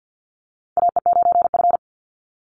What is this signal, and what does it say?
Morse code, heard as a keyed tone, about 50 wpm, 700 Hz. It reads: AE9F
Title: NA SOTA Chaser Callsigns 50wpm